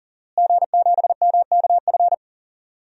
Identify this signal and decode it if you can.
Morse code, heard as a keyed tone, about 40 wpm, 700 Hz. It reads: G7MKF
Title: DXCC-100 Call Signs - 5 Characters - Repeated 2x - 40wpm